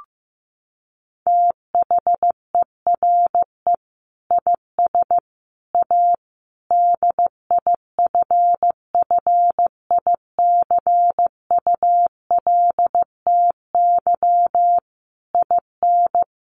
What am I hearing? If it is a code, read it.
Morse code, heard as a keyed tone, about 15 wpm, 700 Hz. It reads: THERE IS A DIFFICULTY IN
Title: Sentences from Top 1500 Words 15wpm